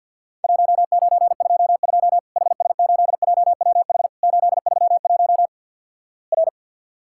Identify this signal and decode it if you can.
Morse code, heard as a keyed tone, about 50 wpm, 700 Hz. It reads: 1922 HS8JWH 831 R
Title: Worked All Europe DX Contest - QTC Exchange - 50wpm